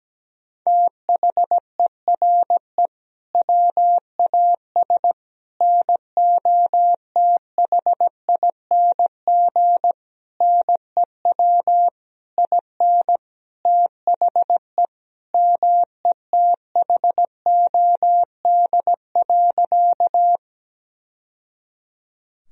Morse code, heard as a keyed tone, about 17 wpm, 700 Hz. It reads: THERE WAS NOTHING NEW IN THE METHOD.
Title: Sentences from Top 1000 Words - Review Part 3 - 17wpm